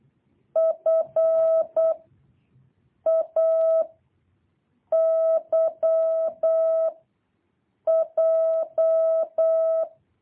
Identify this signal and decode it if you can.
Morse code, heard as a keyed tone, about 8 wpm, 650 Hz. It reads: F A Y J
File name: jh8cbh / 2023cw